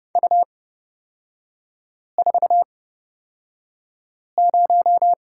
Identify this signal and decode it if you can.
Morse code, heard as a keyed tone, about 30 wpm, spaced as for 5 wpm, 700 Hz. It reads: U40
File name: Lesson 38e - Letter q - Three Characters 30wpm